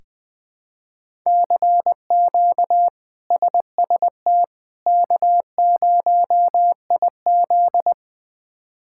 Morse code, heard as a keyed tone, about 20 wpm, 700 Hz. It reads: CQ SST K0IZ
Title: K1USN SST QSO Practice - 20wpm